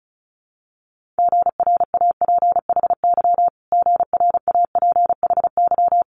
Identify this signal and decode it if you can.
Morse code, heard as a keyed tone, about 34 wpm, 700 Hz. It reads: GRAPHY GRAPHY
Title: Multi-letter Phonemes Repeated 3x 35wpm